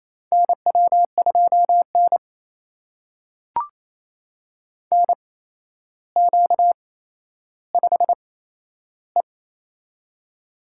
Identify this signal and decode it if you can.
Morse code, heard as a keyed tone, about 27 wpm, spaced as for 12 wpm, 700 Hz. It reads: NW2N NQ5E
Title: USA Call Signs - 4 Characters - 8x Character Spacing - 28wpm